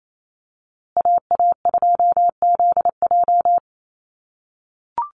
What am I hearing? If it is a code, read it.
Morse code, heard as a keyed tone, about 28 wpm, 700 Hz. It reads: AA2ZJ
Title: USA Call Signs Repeated 2x 28wpm